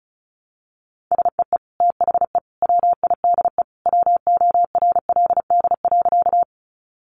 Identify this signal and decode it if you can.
Morse code, heard as a keyed tone, about 34 wpm, 700 Hz. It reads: SEE THE WIDE WORLD.
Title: Sentences from Top 900 Words 35wpm